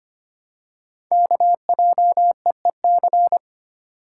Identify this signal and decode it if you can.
Morse code, heard as a keyed tone, about 25 wpm, 690 Hz. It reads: KJEEC
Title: Code Groups - 5 Letters - 25wpm